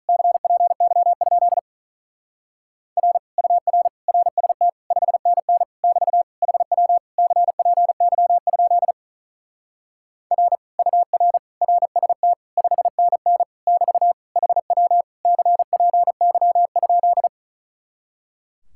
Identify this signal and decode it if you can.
Morse code, heard as a keyed tone, about 44 wpm, 700 Hz. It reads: CPY? R UR RST 5NN <BT> HW CPY? R UR RST 5NN <BT> HW CPY?